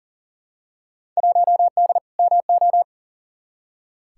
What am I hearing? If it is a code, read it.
Morse code, heard as a keyed tone, about 40 wpm, 700 Hz. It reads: 1D MO
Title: Field Day Exchanges - 40wpm